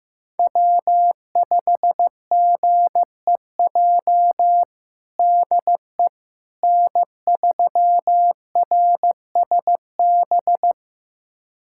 Morse code, heard as a keyed tone, about 15 wpm, 700 Hz. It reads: W5GEJ DE N3RSB
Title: USA Callsign DE Callsign - 5 Characters - 15wpm